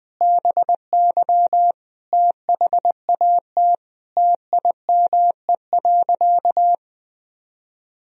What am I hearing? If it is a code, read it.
Morse code, heard as a keyed tone, about 20 wpm, 700 Hz. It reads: BY THAT TIME.
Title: Sentences from Top 1000 Words - Review Part 2 - 20wpm